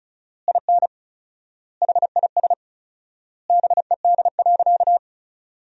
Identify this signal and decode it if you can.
Morse code, heard as a keyed tone, about 36 wpm, 700 Hz. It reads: IN HIS BED.
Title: 4x Word-Spacing - Sentences from Top 500 Words - 35wpm